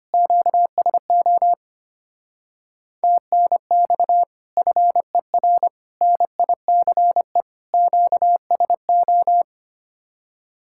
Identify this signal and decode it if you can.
Morse code, heard as a keyed tone, about 33 wpm, 700 Hz. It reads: QSO TNX FER NICE QSO